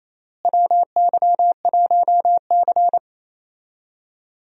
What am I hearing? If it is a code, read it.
Morse code, heard as a keyed tone, about 27 wpm, 700 Hz. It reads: WY1C
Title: USA Call Signs - 4 Characters - Repeated 3x - 28wpm